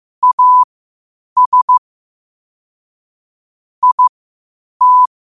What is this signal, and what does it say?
Morse code, heard as a keyed tone, about 15 wpm, spaced as for 8 wpm, 1000 Hz. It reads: AS IT
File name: qotdfiles / 8wpm